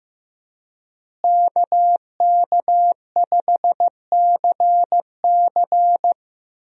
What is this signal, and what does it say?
Morse code, heard as a keyed tone, about 15 wpm, 700 Hz. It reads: KK5CC
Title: USA Call Signs - 5 Characters - Repeated 3x - 15wpm